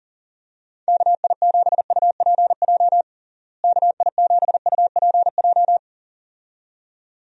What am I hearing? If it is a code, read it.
Morse code, heard as a keyed tone, about 40 wpm, 700 Hz. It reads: KI7UPJ KI7UPJ
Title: USA Call Signs - 6 Characters - Repeated 2x - 40wpm